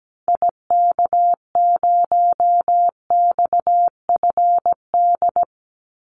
Morse code, heard as a keyed tone, about 17 wpm, 700 Hz. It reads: IK0XFD
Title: Call Signs - 17wpm